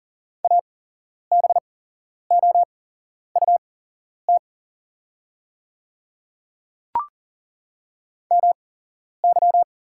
Morse code, heard as a keyed tone, about 40 wpm, spaced as for 11 wpm, 700 Hz. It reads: ABOUT MY
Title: 8x Character Spacing - Top 100 Words - 40wpm